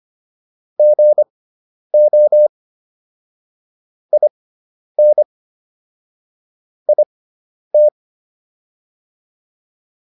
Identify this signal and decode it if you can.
Morse code, heard as a keyed tone, about 25 wpm, spaced as for 10 wpm, 600 Hz. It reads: GO IN IT